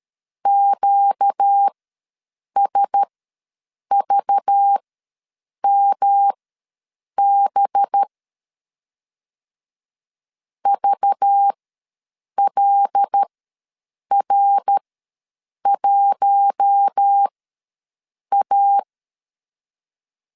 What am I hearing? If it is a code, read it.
Morse code, heard as a keyed tone, about 13 wpm, spaced as for 7 wpm, 800 Hz. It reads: QSVMB VLR1A
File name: LEZIONI / 5 LBVC / 5 LBV ws 12wpm 3015